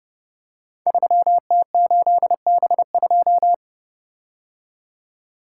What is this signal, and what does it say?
Morse code, heard as a keyed tone, about 30 wpm, 700 Hz. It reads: 3T8B2
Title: Code Groups - 5 Letter:Numbers - 30wpm